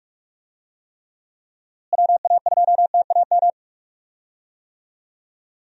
Morse code, heard as a keyed tone, about 46 wpm, 700 Hz. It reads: WA2TAM
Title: USA Call Signs - 6 Characters - Repeated 3x - 45wpm